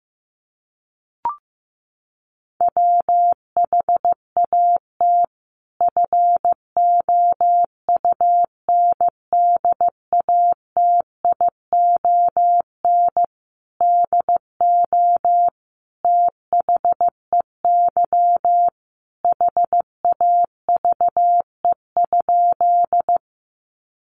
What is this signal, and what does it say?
Morse code, heard as a keyed tone, about 15 wpm, 700 Hz. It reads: WHAT FOUNDATION DO THEY HAVE?
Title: Sentences from Top 1200 Words 15wpm